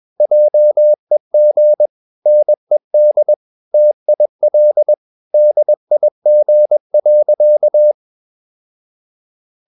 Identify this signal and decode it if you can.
Morse code, heard as a keyed tone, about 21 wpm, 600 Hz. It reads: JEG NED TIL DIG.
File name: Ferie QRS 21wpm